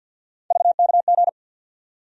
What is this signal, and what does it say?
Morse code, heard as a keyed tone, about 53 wpm, 700 Hz. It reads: VKG
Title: USA Call Sign Postfixes - 3 Letters 50wpm